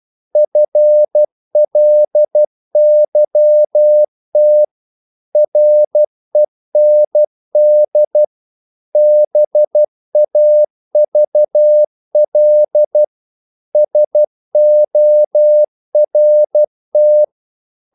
Morse code, heard as a keyed tone, about 12 wpm, 600 Hz. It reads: FLYT REND BAVL SORT
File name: Korte ord DK 12wpm